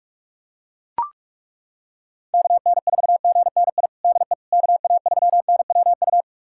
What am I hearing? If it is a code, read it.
Morse code, heard as a keyed tone, about 41 wpm, 700 Hz. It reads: KN4GNI DE KA3NWU
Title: USA Callsign DE Callsign - 6 Characters - 45wpm